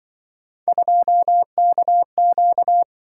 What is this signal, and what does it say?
Morse code, heard as a keyed tone, about 23 wpm, 700 Hz. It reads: 2KQ
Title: Speed-Racing - Three Letter-Number Combinations - 20wpm